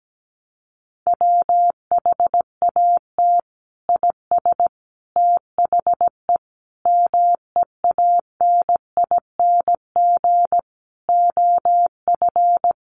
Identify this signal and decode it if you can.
Morse code, heard as a keyed tone, about 17 wpm, 700 Hz. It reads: WHAT IS THE MEANING OF
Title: Sentences from Top 900 Words 17wpm